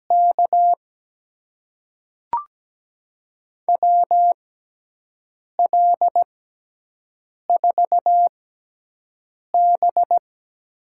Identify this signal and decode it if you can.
Morse code, heard as a keyed tone, about 17 wpm, spaced as for 6 wpm, 700 Hz. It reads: K WL4B
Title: USA Call Signs - 6x Character Spacing - 17wpm